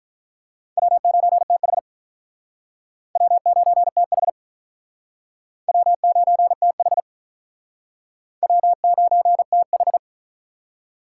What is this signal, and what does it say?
Morse code, heard as a keyed tone, about 46 wpm, 700 Hz. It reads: W9TH W9TH W9TH W9TH